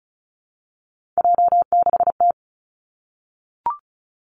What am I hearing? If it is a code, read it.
Morse code, heard as a keyed tone, about 34 wpm, 700 Hz. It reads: J6T
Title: Three Challenging Letter-Number Combinations 35wpm